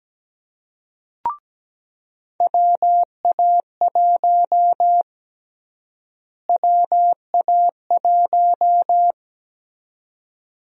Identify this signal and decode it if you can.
Morse code, heard as a keyed tone, about 17 wpm, 700 Hz. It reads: WA1 WA1
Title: USA Call Sign Prefixes - Repeated 2x - 17wpm